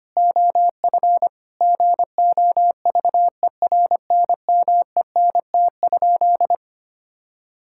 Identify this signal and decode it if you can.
Morse code, heard as a keyed tone, about 25 wpm, 700 Hz. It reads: OF GOVERNMENT?
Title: Sentences from Top 1000 Words - Review Part 5 - 25wpm